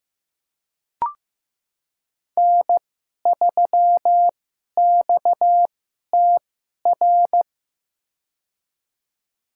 Morse code, heard as a keyed tone, about 15 wpm, spaced as for 11 wpm, 700 Hz. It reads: N3XTR
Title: USA Call Signs - 5 Characters - 2x Character Spacing - 15wpm